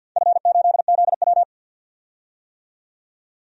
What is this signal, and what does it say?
Morse code, heard as a keyed tone, about 52 wpm, 700 Hz. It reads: F8ZW
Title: DXCC-100 Call Signs - 4 Characters - 50wpm